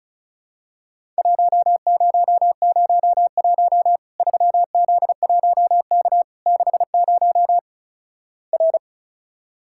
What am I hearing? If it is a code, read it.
Morse code, heard as a keyed tone, about 36 wpm, 700 Hz. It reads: 1001 3Z1K 60 R
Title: Worked All Europe DX Contest - QTC Exchange - 35wpm